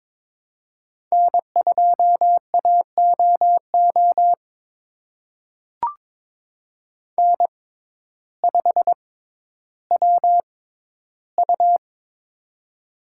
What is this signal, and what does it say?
Morse code, heard as a keyed tone, about 22 wpm, spaced as for 12 wpm, 700 Hz. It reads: N2AOO N5WU
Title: USA Call Signs - 6x Character Spacing - 22wpm